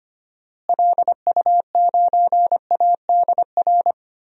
Morse code, heard as a keyed tone, about 25 wpm, 700 Hz. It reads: LU9ADR
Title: DXCC-100 Call Signs - 6 Characters - Repeated 2x - 25wpm